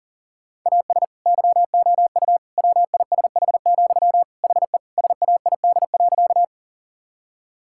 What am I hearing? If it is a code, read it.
Morse code, heard as a keyed tone, about 37 wpm, 700 Hz. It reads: AS YOU WISH, HE SAID.
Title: Sentences from Top 1600 Words 40wpm